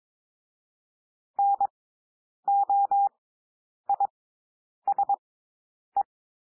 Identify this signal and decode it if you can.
Morse code, heard as a keyed tone, about 22 wpm, spaced as for 9 wpm, 800 Hz. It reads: NOISE
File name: SOUND WORDS 22 wpm